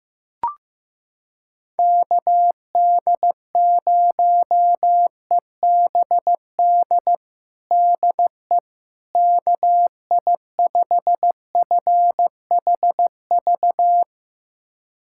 Morse code, heard as a keyed tone, about 15 wpm, 700 Hz. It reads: KD0EBD DE KI5FHV
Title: USA Callsign DE Callsign - 6 Characters - 15wpm